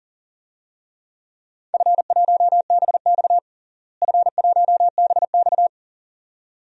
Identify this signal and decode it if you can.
Morse code, heard as a keyed tone, about 37 wpm, 700 Hz. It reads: F1BX F1BX
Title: DXCC-100 Call Signs - 4 Characters - Repeated 2x - 40wpm